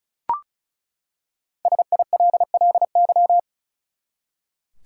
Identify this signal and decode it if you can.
Morse code, heard as a keyed tone, about 36 wpm, 700 Hz. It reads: SILLY
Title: Five Letter Words - 35wpm